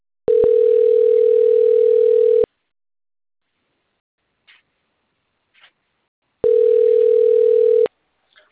Ringback tone (North America), 440 + 480 Hz.